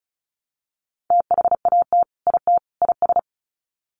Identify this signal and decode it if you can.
Morse code, heard as a keyed tone, about 34 wpm, 700 Hz. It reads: THAT IT IS
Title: Sets of 3 Words 35wpm